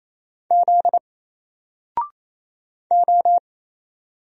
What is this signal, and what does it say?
Morse code, heard as a keyed tone, about 27 wpm, 700 Hz. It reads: Z O
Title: Single Letter-Number 28wpm